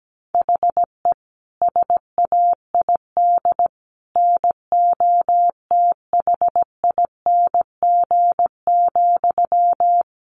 Morse code, heard as a keyed tone, about 17 wpm, 700 Hz. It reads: HE SAID NOTHING,
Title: Sentences from Top 1500 Words 17wpm